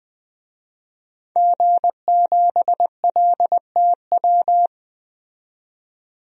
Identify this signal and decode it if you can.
Morse code, heard as a keyed tone, about 20 wpm, 700 Hz. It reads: G7LTW
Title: DXCC-100 Call Signs - 5 Characters - 20wpm